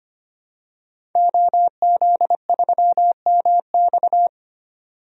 Morse code, heard as a keyed tone, about 25 wpm, 700 Hz. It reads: OZ3MX
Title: DXCC-100 Call Signs - 5 Characters - 25wpm